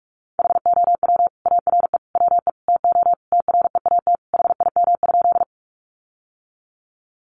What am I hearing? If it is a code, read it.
Morse code, heard as a keyed tone, about 44 wpm, 700 Hz. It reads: HOW ARE WE TO TREAT HIM?
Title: Sentences from Top 1300 Words 45wpm